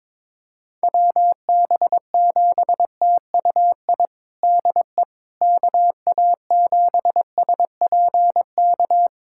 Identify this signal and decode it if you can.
Morse code, heard as a keyed tone, about 22 wpm, 700 Hz. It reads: WB7TUI DE KA7SPK
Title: USA Callsign DE Callsign 22wpm